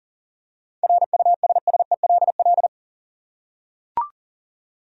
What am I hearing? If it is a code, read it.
Morse code, heard as a keyed tone, about 40 wpm, 700 Hz. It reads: RUSSELL
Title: USA Common Names - 40wpm